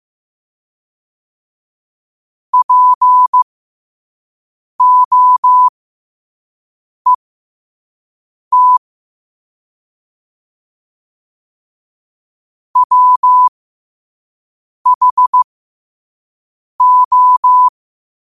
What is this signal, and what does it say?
Morse code, heard as a keyed tone, about 15 wpm, spaced as for 5 wpm, 1000 Hz. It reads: POET WHO